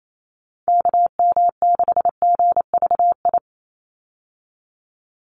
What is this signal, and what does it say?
Morse code, heard as a keyed tone, about 28 wpm, 700 Hz. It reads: KM6GVI
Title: USA Call Signs 28wpm